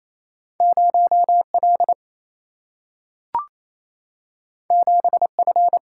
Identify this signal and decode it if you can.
Morse code, heard as a keyed tone, about 28 wpm, 700 Hz. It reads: 0L 7F
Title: Single-Letter Single-Number 28wpm